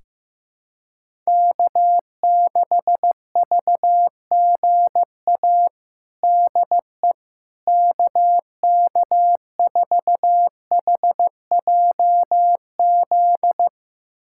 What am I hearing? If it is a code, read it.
Morse code, heard as a keyed tone, about 15 wpm, 700 Hz. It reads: K6VGA DE KK4HJZ
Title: USA Callsign DE Callsign 15wpm